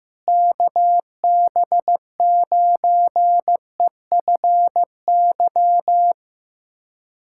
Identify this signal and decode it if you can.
Morse code, heard as a keyed tone, about 15 wpm, 700 Hz. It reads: KB9EFY
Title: USA Callsign DE Callsign 15wpm